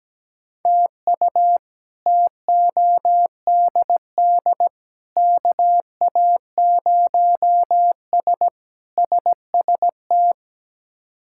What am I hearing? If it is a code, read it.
Morse code, heard as a keyed tone, about 17 wpm, 700 Hz. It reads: TU TODD KA0S SST
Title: K1USN SST QSO Practice - 17wpm